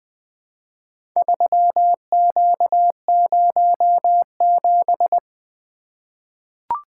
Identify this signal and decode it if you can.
Morse code, heard as a keyed tone, about 20 wpm, 700 Hz. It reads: 3Q07
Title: Four Letter-Number Combinations - 20wpm